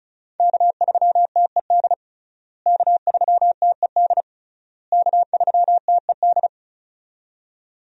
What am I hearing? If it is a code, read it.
Morse code, heard as a keyed tone, about 36 wpm, 700 Hz. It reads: K3TED K3TED K3TED
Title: USA Call Signs - 5 Characters - Repeated 3x - 35wpm